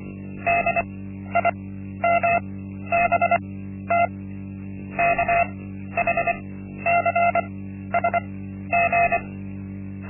Morse code, heard as a keyed tone, about 24 wpm, spaced as for 12 wpm, 690 Hz. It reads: DIMBT KHCSG
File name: Linksmakalnis-1